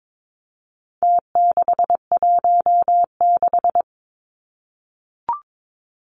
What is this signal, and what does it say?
Morse code, heard as a keyed tone, about 22 wpm, 700 Hz. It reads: T616
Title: Numbers - Contest Style - 4-Digit 22wpm